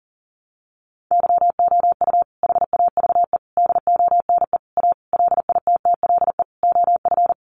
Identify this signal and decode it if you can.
Morse code, heard as a keyed tone, about 37 wpm, 700 Hz. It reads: YOU HAVE DONE A LITTLE OF
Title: Sentences from Top 700 Words 40wpm